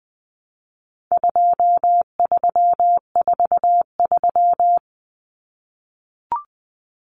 Morse code, heard as a keyed tone, about 20 wpm, 700 Hz. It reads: 2343
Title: Numbers - 4-Digit 20wpm